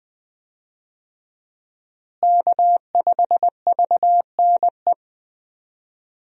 Morse code, heard as a keyed tone, about 20 wpm, 700 Hz. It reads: K5VNE
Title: USA Call Signs - 5 Characters - Repeated 2x - 20wpm